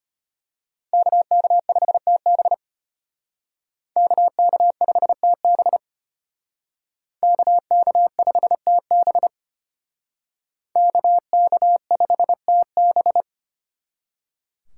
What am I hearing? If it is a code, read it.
Morse code, heard as a keyed tone, about 32 wpm, 700 Hz. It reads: KK5TB KK5TB KK5TB KK5TB